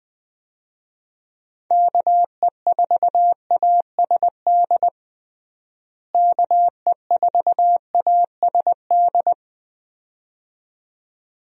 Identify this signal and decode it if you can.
Morse code, heard as a keyed tone, about 19 wpm, 700 Hz. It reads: KE4ASD KE4ASD
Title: USA Call Signs - 6 Characters - Repeated 2x - 20wpm